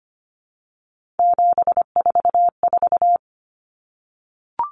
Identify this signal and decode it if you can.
Morse code, heard as a keyed tone, about 25 wpm, 700 Hz. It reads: 744
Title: Numbers - 3-Digit 25wpm